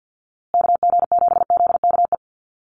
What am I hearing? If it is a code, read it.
Morse code, heard as a keyed tone, about 50 wpm, 700 Hz. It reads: KG7ZKE
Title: USA Call Signs Repeated 3x 50wpm